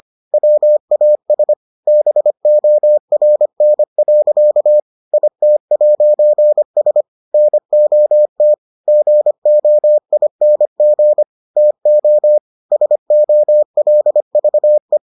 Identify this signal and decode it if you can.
Morse code, heard as a keyed tone, about 25 wpm, 600 Hz. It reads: WAS BORN. IT'S NOT GOING TO SOLVE